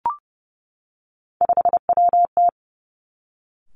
Morse code, heard as a keyed tone, about 30 wpm, 700 Hz. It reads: 5WT
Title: Three Challenging Letter-Number Combinations 30wpm